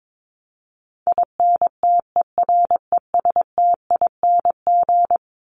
Morse code, heard as a keyed tone, about 22 wpm, 700 Hz. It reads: INTERESTING
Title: ING Suffix - Top 100 Words with Base Words 22wpm